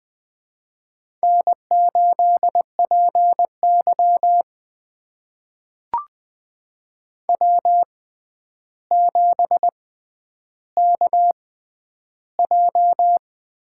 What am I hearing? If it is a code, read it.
Morse code, heard as a keyed tone, about 20 wpm, spaced as for 10 wpm, 700 Hz. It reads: N8PY W7KJ
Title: USA Call Signs - 4 Characters - 6x Character Spacing - 20wpm